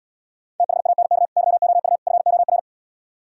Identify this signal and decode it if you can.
Morse code, heard as a keyed tone, about 77 wpm, 700 Hz. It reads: THANK YOU FOR